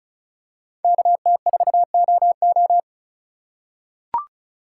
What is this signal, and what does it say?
Morse code, heard as a keyed tone, about 36 wpm, 700 Hz. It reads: KT4OO
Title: USA Call Signs - 3x Character Spacing - 35wpm